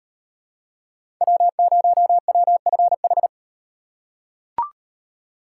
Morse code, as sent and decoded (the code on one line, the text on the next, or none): .-- ----- .-- ..-. ....
W0WFH